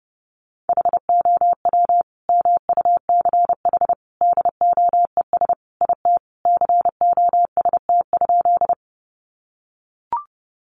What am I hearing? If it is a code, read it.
Morse code, heard as a keyed tone, about 30 wpm, 700 Hz. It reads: HOW MUCH DOES IT COST?
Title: Sentences from Top 800 Words 30wpm